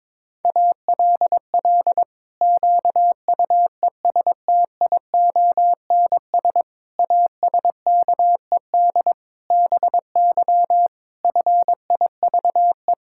Morse code, heard as a keyed tone, about 22 wpm, 700 Hz. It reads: ALL QUESTIONS ASKED BY FIVE
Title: Pangram Challenge v2 - 22wpm